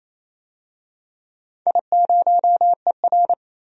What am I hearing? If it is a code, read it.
Morse code, heard as a keyed tone, about 28 wpm, 700 Hz. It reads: I0ER
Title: DXCC-100 Call Signs - 4 Characters - 28wpm